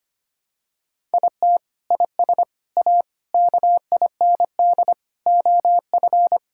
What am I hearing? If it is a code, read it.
Morse code, heard as a keyed tone, about 25 wpm, 700 Hz. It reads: IT IS A KIND OF
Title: Sentences from Top 1000 Words - Review Part 5 - 25wpm